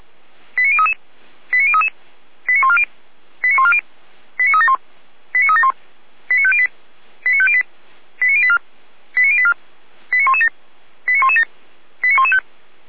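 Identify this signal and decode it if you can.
Five-tone selective call (ZVEI), heard as quick runs of five tones, tones of about 70 ms. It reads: 89033 89033 89140 89140 89371 89371 89508 89508 89084 89084 89117 89117 89115
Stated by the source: ZVEI five-tone selective call received by SDR.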